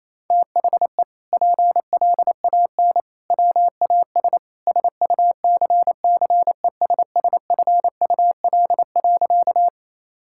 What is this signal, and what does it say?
Morse code, heard as a keyed tone, about 28 wpm, 700 Hz. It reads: THE PLAN WAS SUCCESSFUL.
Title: Sentences from Top 1000 Words - Review Part 6 - 28wpm